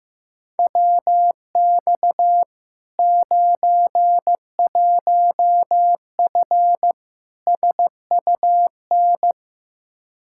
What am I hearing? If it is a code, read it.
Morse code, heard as a keyed tone, about 15 wpm, 700 Hz. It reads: WX 91F SUN
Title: Practice QSO - Exchange 2 - Info - 15wpm